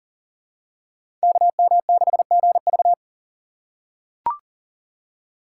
Morse code, heard as a keyed tone, about 40 wpm, 700 Hz. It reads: KM6GV
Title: USA Call Signs - 5 Characters - 2x Character Spacing - 40wpm